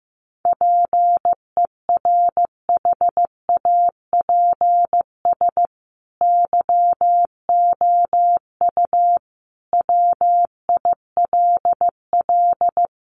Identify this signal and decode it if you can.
Morse code, heard as a keyed tone, about 15 wpm, 700 Hz. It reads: PERHAPS YOU WILL
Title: Sentences from Top 1400 Words 15wpm